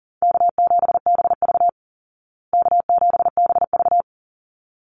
Morse code, heard as a keyed tone, about 40 wpm, 700 Hz. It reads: K7BV K7BV
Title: Call Signs Repeated 3x - 40wpm